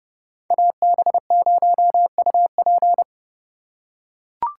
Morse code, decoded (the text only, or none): AB0UP